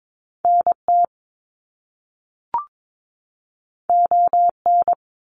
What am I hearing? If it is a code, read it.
Morse code, heard as a keyed tone, about 22 wpm, 700 Hz. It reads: NT ON
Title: Canadian Provinces and Territories 22wpm